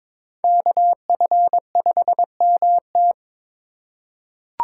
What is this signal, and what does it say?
Morse code, heard as a keyed tone, about 22 wpm, 700 Hz. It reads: KF5MT
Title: USA Call Signs - 5 Characters - 2x Character Spacing - 22wpm